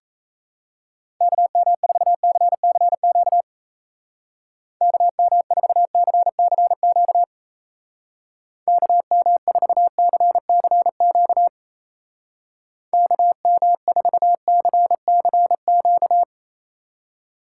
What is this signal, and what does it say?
Morse code, heard as a keyed tone, about 36 wpm, 700 Hz. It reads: KM4CCQ KM4CCQ KM4CCQ KM4CCQ